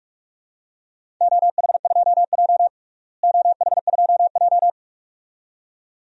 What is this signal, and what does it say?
Morse code, heard as a keyed tone, about 42 wpm, 700 Hz. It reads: OH2J OH2J
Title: DXCC-100 Call Signs - 4 Characters - Repeated 2x - 45wpm